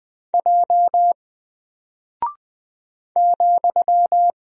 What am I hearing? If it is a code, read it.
Morse code, heard as a keyed tone, about 20 wpm, 700 Hz. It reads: J ,
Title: Single Character (Letter:Number:Punctuation:Prosign) 20wpm